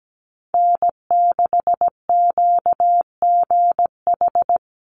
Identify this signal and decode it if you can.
Morse code, heard as a keyed tone, about 17 wpm, 700 Hz. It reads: N6QGH
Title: USA Call Signs 17wpm